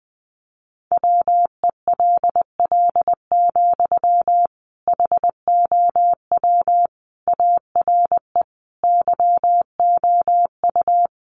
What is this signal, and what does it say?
Morse code, heard as a keyed tone, about 20 wpm, 700 Hz. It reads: WELL, HOW ARE YOU
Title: Sentences from Top 1200 Words 20wpm